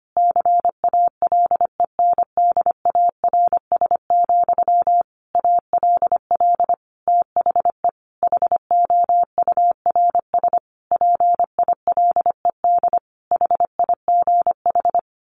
Morse code, heard as a keyed tone, about 25 wpm, 700 Hz. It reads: CALENDARS, ALL THE HOURS PILED HIGH
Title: A Sound Of Thunder 25wpm